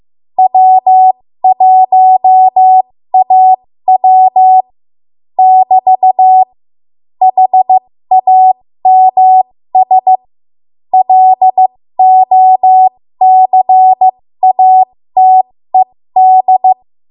Morse code, heard as a keyed tone, about 15 wpm, spaced as for 13 wpm, 750 Hz. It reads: W1AW <BT> HAMS LOCATED
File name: Morse / 13WPM